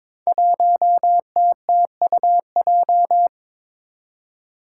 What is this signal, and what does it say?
Morse code, heard as a keyed tone, about 22 wpm, 700 Hz. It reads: 1TTUJ
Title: Five Challenging Letter-Number Combinations 22wpm